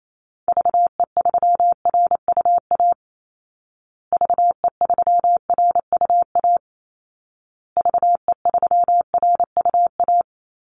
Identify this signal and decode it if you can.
Morse code, heard as a keyed tone, about 28 wpm, 700 Hz. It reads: VE3RUA VE3RUA VE3RUA
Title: Call Signs Repeated 2x - 28wpm